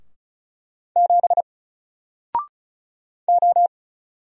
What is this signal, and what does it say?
Morse code, heard as a keyed tone, about 36 wpm, 700 Hz. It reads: 7 O
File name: Single-Letter-Number-35wpm